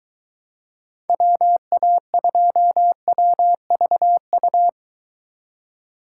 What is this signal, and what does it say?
Morse code, heard as a keyed tone, about 22 wpm, 700 Hz. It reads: WA2WVU